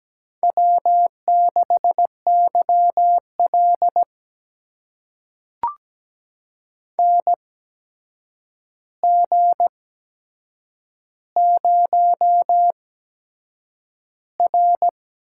Morse code, heard as a keyed tone, about 17 wpm, spaced as for 7 wpm, 700 Hz. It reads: W6YL NG0R
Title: USA Call Signs - 4 Characters - 8x Character Spacing - 17wpm